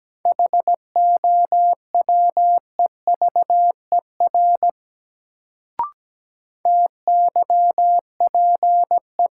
Morse code, heard as a keyed tone, about 17 wpm, 700 Hz. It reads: HOWEVER TYPE
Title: Sentences from Top 600 Words - New Word Drill - 17wpm